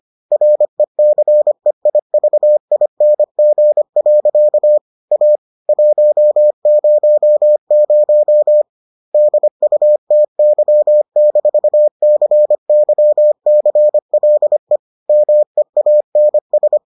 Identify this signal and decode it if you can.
Morse code, heard as a keyed tone, about 25 wpm, 600 Hz. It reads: RECEIVING. A 100 DUTY-CYCLE MEANS